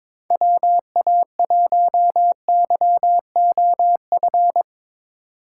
Morse code, heard as a keyed tone, about 22 wpm, 700 Hz. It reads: WA1YOF
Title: USA Call Signs - 6 Characters - 8x Character Spacing - 22wpm